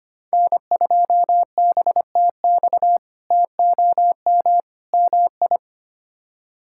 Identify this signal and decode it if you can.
Morse code, heard as a keyed tone, about 25 wpm, 700 Hz. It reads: N2BTX TOM MI
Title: K1USN SST QSO Practice - 25wpm